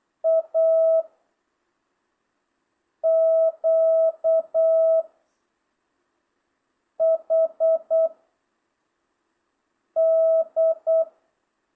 Morse code, heard as a keyed tone, about 8 wpm, 650 Hz. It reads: A Q H D